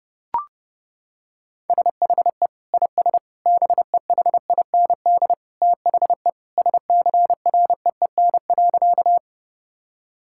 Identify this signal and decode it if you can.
Morse code, heard as a keyed tone, about 30 wpm, 700 Hz. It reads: SHE IS BEHIND THE SCREEN.
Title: Sentences from Top 1000 Words - Review Part 6 - 30wpm